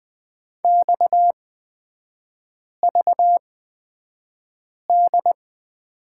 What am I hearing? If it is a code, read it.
Morse code, heard as a keyed tone, about 20 wpm, spaced as for 5 wpm, 690 Hz. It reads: XVD
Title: 229 Lesson 37e - Letter x - Three Characters 20wpm